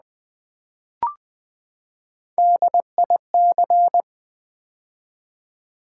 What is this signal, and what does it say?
Morse code, heard as a keyed tone, about 20 wpm, 700 Hz. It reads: DIC
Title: Build A Word - 3 Letters At A Time - 8-Letter Words - 20wpm